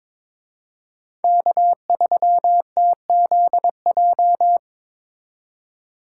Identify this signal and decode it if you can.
Morse code, heard as a keyed tone, about 25 wpm, 700 Hz. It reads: K3TZJ
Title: Speed Racing - USA 1x3 Call Signs 22wpm